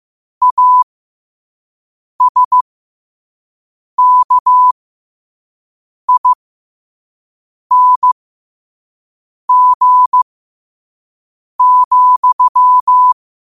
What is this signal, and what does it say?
Morse code, heard as a keyed tone, about 15 wpm, spaced as for 5 wpm, 1000 Hz. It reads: ASKING,
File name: qotdfiles / 5wpm